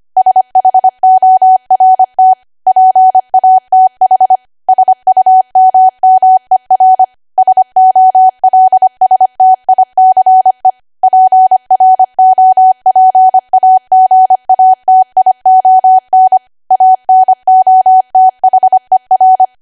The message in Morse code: SHORT PATH SUMMER SOLSTICE PROPAGATION ANOTHER